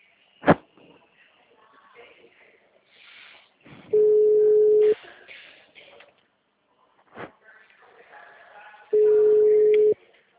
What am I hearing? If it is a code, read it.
Ringback tone (Europe, 425 Hz).
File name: out- -2007- - - .48185